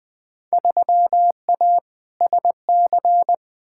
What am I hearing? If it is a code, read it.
Morse code, heard as a keyed tone, about 20 wpm, 700 Hz. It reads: 3A SC
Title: Field Day Exchanges Encore - 20wpm